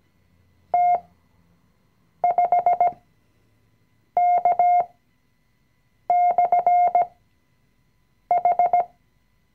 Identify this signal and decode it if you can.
Morse code, heard as a keyed tone, about 17 wpm, spaced as for 6 wpm, 710 Hz. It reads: T5K/H
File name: CW-Online-Kurs / Anfänger / Lektion 020